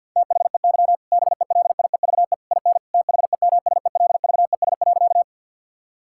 Morse code, heard as a keyed tone, about 50 wpm, 700 Hz. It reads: THEY BELIEVE IN THEMSELVES.
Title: Sentences from Top 1000 Words - Review Part 5 - 50wpm